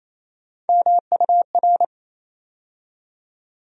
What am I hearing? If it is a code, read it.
Morse code, heard as a keyed tone, about 28 wpm, 700 Hz. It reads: MUR
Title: USA Call Sign Postfixes - 3 Letters 28wpm